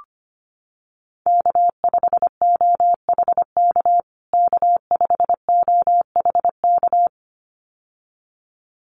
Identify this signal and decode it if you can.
Morse code, heard as a keyed tone, about 25 wpm, 700 Hz. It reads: K5OHK K5OHK
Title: USA Call Signs Repeated 2x 25wpm